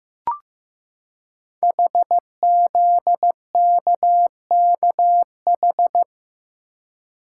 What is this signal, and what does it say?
Morse code, heard as a keyed tone, about 15 wpm, 700 Hz. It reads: HZKKH
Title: Code Groups - 5 Letters - 15wpm